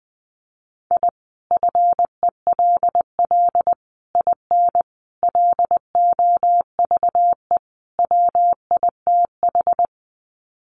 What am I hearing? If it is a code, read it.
Morse code, heard as a keyed tone, about 20 wpm, 700 Hz. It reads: I FELL IN LOVE WITH
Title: Sets of 5 Words Encore 20wpm